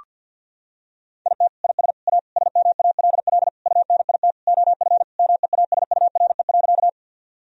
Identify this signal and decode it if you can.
Morse code, heard as a keyed tone, about 50 wpm, 700 Hz. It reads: IT IS A SMALL UNIT OF MEASURE.
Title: Sentences from Top 1000 Words - Review Part 2 - 50wpm